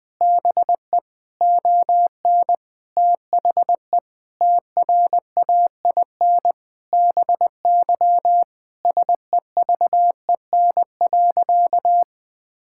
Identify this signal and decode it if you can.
Morse code, heard as a keyed tone, about 20 wpm, 700 Hz. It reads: BE ON THE TRAIN BY SEVEN.
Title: Sentences with 4-Word Phrases - 20wpm